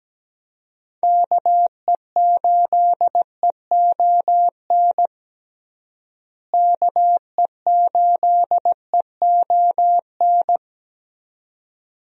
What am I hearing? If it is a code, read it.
Morse code, heard as a keyed tone, about 17 wpm, 700 Hz. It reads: KE8EON KE8EON
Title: USA Call Signs - 6 Characters - Repeated 2x - 17wpm